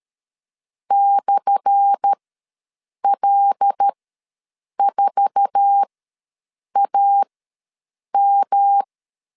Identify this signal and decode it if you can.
Morse code, heard as a keyed tone, about 13 wpm, spaced as for 7 wpm, 800 Hz. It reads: /L4AM
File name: LEZIONI / 10 SIGNS / 10A SLASH ws 12wpm 3443